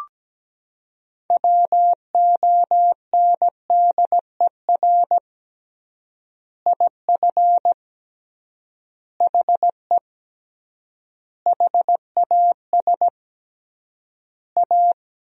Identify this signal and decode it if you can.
Morse code, heard as a keyed tone, about 17 wpm, 700 Hz. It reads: WONDER IF HE HAS A
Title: 3x Word-Spacing - Sentences from Top 500 Words - 17wpm